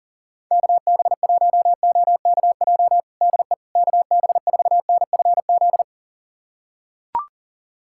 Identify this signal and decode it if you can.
Morse code, heard as a keyed tone, about 40 wpm, 700 Hz. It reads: KB1OKJ DE KB4NFZ
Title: USA Callsign DE Callsign 40wpm